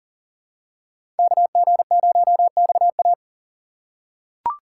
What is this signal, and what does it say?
Morse code, heard as a keyed tone, about 40 wpm, 700 Hz. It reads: KG0XA
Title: USA Call Signs - 5 Characters - 3x Character Spacing - 40wpm